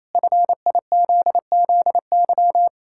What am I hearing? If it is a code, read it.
Morse code, heard as a keyed tone, about 28 wpm, 700 Hz. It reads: FIZZY
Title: Pangram Challenge v2 - 28wpm